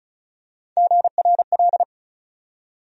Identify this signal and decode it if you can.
Morse code, heard as a keyed tone, about 34 wpm, 700 Hz. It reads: GRL
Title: USA Call Sign Postfixes - 3 Letters 35wpm